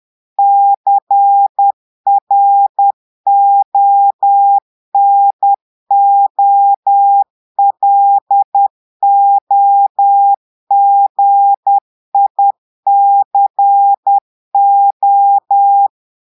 Morse code, heard as a keyed tone, about 10 wpm, 800 Hz. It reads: CRONOLOGICO
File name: Portugais - texte en clair 1 - vitesse - 10 mots minute